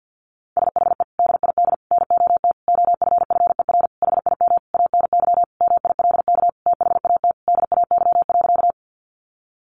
Morse code, heard as a keyed tone, about 50 wpm, 700 Hz. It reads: SHE DID NOT OFFER HIM ANY MILK THAT DAY.
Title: Sentences from Top 1500 Words 50wpm